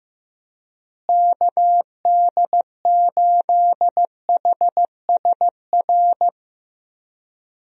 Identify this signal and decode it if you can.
Morse code, heard as a keyed tone, about 15 wpm, 700 Hz. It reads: KD8HSR
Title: USA Call Signs - 6 Characters - Repeated 2x - 15wpm